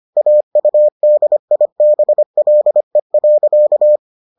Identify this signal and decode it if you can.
Morse code, heard as a keyed tone, about 25 wpm, 600 Hz. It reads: AUDIBLE.